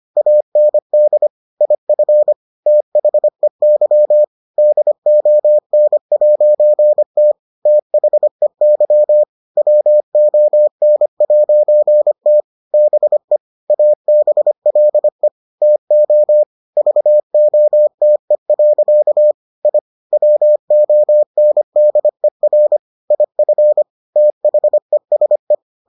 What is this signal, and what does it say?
Morse code, heard as a keyed tone, about 25 wpm, 600 Hz. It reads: AND IF THEY DON'T THEY WON'T BE ABLE TO VOTE. I WONDER IF THESE